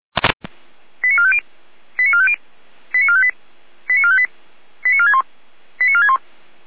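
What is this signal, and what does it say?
Five-tone selective call (ZVEI), heard as quick runs of five tones, tones of about 70 ms. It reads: 89450 89450 89458 89458 89461 89461
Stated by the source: ZVEI five-tone selective call received on SDR.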